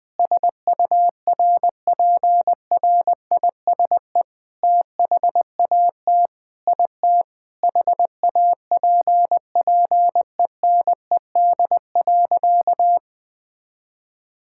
Morse code, heard as a keyed tone, about 21 wpm, 700 Hz. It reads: SURPRISE THAT IT HAPPENED.
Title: Sentences with 8-Word Phrases - Spoken Beginning - 20wpm